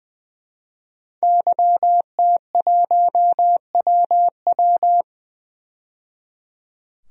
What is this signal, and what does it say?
Morse code, heard as a keyed tone, about 21 wpm, 700 Hz. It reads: YT1WW
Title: DXCC-100 Call Signs - 5 Characters - 20wpm